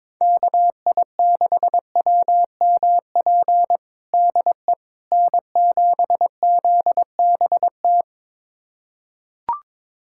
Morse code, heard as a keyed tone, about 22 wpm, 700 Hz. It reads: KI6WMP DE N7ZBT
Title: USA Callsign DE Callsign 22wpm